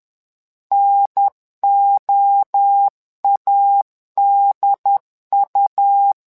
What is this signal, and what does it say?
Morse code, heard as a keyed tone, about 11 wpm, 800 Hz. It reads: NOADU
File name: LEZIONI / 3 MWRO / 3 MWRO 10wpm 2148